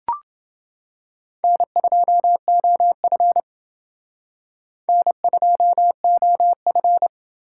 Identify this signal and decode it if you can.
Morse code, heard as a keyed tone, about 28 wpm, 700 Hz. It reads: N2OF N2OF